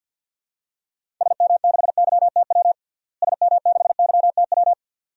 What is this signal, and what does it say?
Morse code, heard as a keyed tone, about 47 wpm, 700 Hz. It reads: SM6YTW SM6YTW
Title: DXCC-100 Call Signs - 6 Characters - Repeated 2x - 50wpm